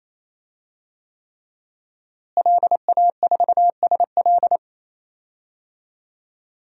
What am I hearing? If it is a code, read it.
Morse code, heard as a keyed tone, about 28 wpm, 700 Hz. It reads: LA4SL
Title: DXCC-100 Call Signs - 5 Characters - Repeated 3x - 28wpm